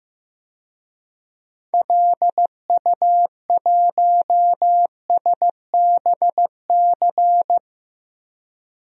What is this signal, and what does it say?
Morse code, heard as a keyed tone, about 16 wpm, 700 Hz. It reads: LU1SBC
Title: DXCC-100 Call Signs - 6 Characters - 15wpm